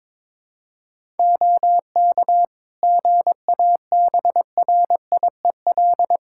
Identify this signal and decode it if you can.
Morse code, heard as a keyed tone, about 21 wpm, 700 Hz. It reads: OK GABRIEL
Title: Practice QSO - Exchange 2 - Info - 22wpm